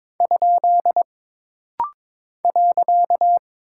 Morse code, heard as a keyed tone, about 22 wpm, 700 Hz. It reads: ? .
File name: practice / 22wpm / Single Punctuation-Prosign - Rapid-Fire - 22wpm